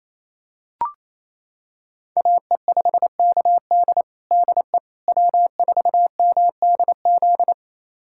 Morse code, heard as a keyed tone, about 28 wpm, 700 Hz. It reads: AE5KD DE W4MDZ
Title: USA Callsign DE Callsign - 5 Characters - 28wpm